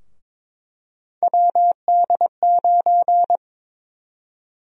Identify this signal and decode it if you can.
Morse code, heard as a keyed tone, about 21 wpm, 700 Hz. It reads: WD9
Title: USA Call Sign Prefixes - Repeated 2x - 22wpm